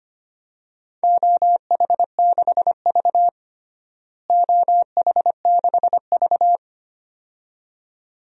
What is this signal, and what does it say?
Morse code, heard as a keyed tone, about 25 wpm, 700 Hz. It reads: OH6V OH6V
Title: DXCC-100 Call Signs - 4 Characters - Repeated 2x - 25wpm